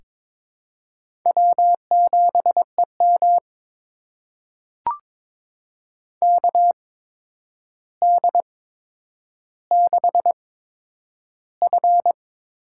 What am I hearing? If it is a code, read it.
Morse code, heard as a keyed tone, about 22 wpm, spaced as for 9 wpm, 700 Hz. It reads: W7EM KD6F
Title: USA Call Signs - 4 Characters - 8x Character Spacing - 22wpm